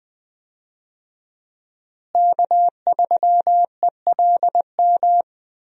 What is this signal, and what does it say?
Morse code, heard as a keyed tone, about 21 wpm, 700 Hz. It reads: K3ELM
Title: USA Call Signs - 5 Characters - Repeated 2x - 20wpm